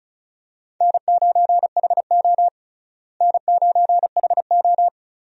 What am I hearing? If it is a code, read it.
Morse code, heard as a keyed tone, about 36 wpm, 700 Hz. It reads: N9HO N9HO
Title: USA Call Signs - 4 Characters - Repeated 2x - 35wpm